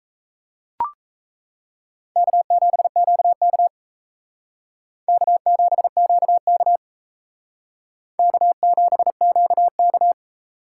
Morse code, heard as a keyed tone, about 36 wpm, 700 Hz. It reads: K7QK K7QK K7QK